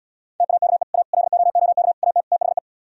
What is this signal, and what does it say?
Morse code, heard as a keyed tone, about 77 wpm, 700 Hz. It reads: TAKE A LOOK AT THE